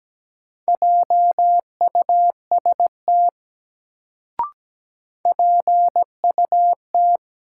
Morse code, heard as a keyed tone, about 17 wpm, 700 Hz. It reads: JUST PUT
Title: Sentences from Top 200 Words - New Word Drill - 17wpm